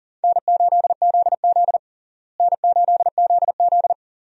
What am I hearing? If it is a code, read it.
Morse code, heard as a keyed tone, about 40 wpm, 700 Hz. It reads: N8ZZ N8ZZ
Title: USA Call Signs - 4 Characters - Repeated 2x - 40wpm